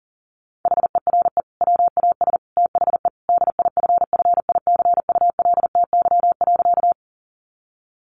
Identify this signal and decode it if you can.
Morse code, heard as a keyed tone, about 40 wpm, 700 Hz. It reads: HERE WAS THE DIFFICULTY.
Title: Sentences from Top 1500 Words 40wpm